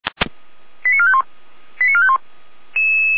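Five-tone selective call (ZVEI), heard as quick runs of five tones, tones of about 70 ms. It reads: 89451 89451
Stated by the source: ZVEI five-tone selective call 88074 received by SDR.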